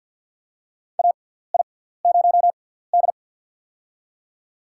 Morse code, heard as a keyed tone, about 53 wpm, spaced as for 17 wpm, 700 Hz. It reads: AI0D
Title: USA Call Signs - 4 Characters - 6x Character Spacing - 50wpm